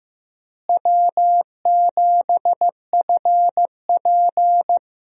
Morse code, heard as a keyed tone, about 15 wpm, 700 Hz. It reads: W7FP
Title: USA Call Signs - 4 Characters - Repeated 3x - 15wpm